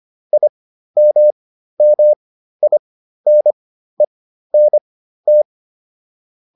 Morse code, heard as a keyed tone, about 25 wpm, spaced as for 13 wpm, 600 Hz. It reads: IMMINENT